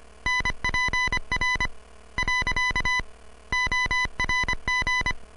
Morse code, heard as a keyed tone, about 25 wpm, spaced as for 19 wpm, 2000 Hz. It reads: NPR.ORG